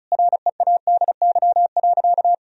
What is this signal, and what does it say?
Morse code, heard as a keyed tone, about 36 wpm, 700 Hz. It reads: READY.
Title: Sentences from Top 1000 Words - Review Part 1 - 35wpm